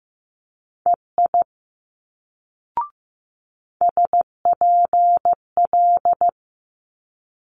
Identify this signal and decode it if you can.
Morse code, heard as a keyed tone, about 15 wpm, 700 Hz. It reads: EI SPL
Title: Multi-letter Phonemes Quick Fire 15wpm